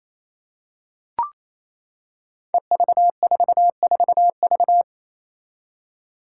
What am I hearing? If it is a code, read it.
Morse code, heard as a keyed tone, about 28 wpm, 700 Hz. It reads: EV44V
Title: Five Challenging Letter-Number Combinations 28wpm